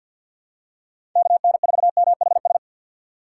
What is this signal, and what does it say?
Morse code, heard as a keyed tone, about 47 wpm, 700 Hz. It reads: KN4MHS
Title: USA Call Signs - 6 Characters - 4x Character Spacing - 50wpm